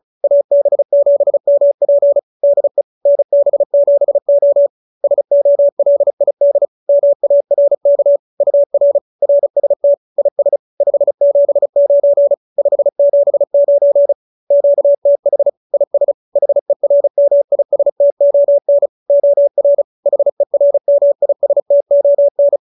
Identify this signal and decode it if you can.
Morse code, heard as a keyed tone, about 34 wpm, 570 Hz. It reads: AB7MP DE NB7O SOLID MARK UR RST IS 579 579 QTH IS HERMISTON OR HERMISTON